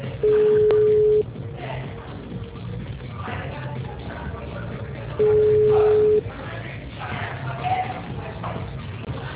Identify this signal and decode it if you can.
Ringback tone (Europe, 425 Hz), with speech.